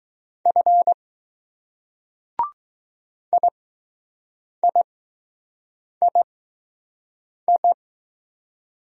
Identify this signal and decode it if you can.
Morse code, heard as a keyed tone, about 20 wpm, spaced as for 6 wpm, 700 Hz. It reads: F IIII